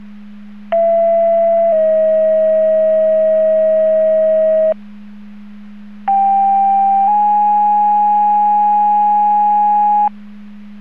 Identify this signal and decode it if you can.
Two-tone page, heard as a one-second tone, then a three-second tone, about 670 then 640 Hz, about 790 then 830 Hz. It is heard from 0.7 to 10.1 s.
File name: Portage 154.385 Paging